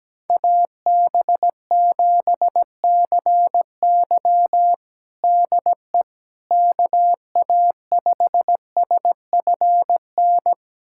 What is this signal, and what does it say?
Morse code, heard as a keyed tone, about 17 wpm, 700 Hz. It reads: AB7CY DE KA5SFN
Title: USA Callsign DE Callsign 17wpm